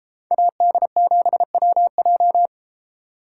Morse code, heard as a keyed tone, about 32 wpm, 700 Hz. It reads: AD7WJ